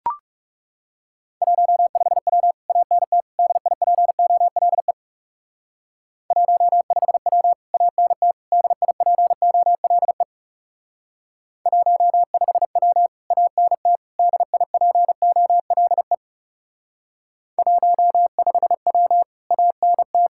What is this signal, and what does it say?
Morse code, heard as a keyed tone, about 36 wpm, 700 Hz. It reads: 15W ANT DIPOLE 15W ANT DIPOLE 15W ANT DIPOLE 15W ANT
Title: Speed-Racing Common QSO Phrases Encore - 30wpm